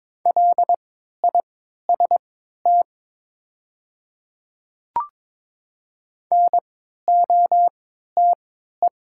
Morse code, heard as a keyed tone, about 22 wpm, spaced as for 12 wpm, 690 Hz. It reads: LIST NOTE
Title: 3x Character Spacing - Top 500 Words - 22wpm